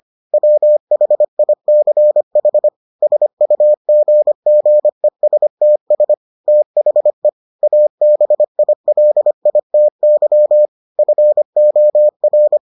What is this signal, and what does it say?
Morse code, heard as a keyed tone, about 25 wpm, 600 Hz. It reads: WHICH SUGGESTS THE ABILITY FOR